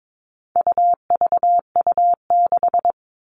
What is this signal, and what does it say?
Morse code, heard as a keyed tone, about 22 wpm, 700 Hz. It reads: UVU6
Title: All Cut Numbers - 4-Digit 22wpm